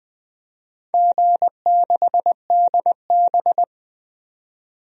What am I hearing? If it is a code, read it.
Morse code, heard as a keyed tone, about 21 wpm, 700 Hz. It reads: G6DB
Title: DXCC-100 Call Signs - 4 Characters - 20wpm